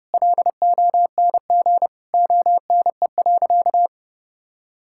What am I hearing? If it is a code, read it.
Morse code, heard as a keyed tone, about 30 wpm, 700 Hz. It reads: LONG ONE.
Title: Sentences from Top 1000 Words - Review Part 5 - 30wpm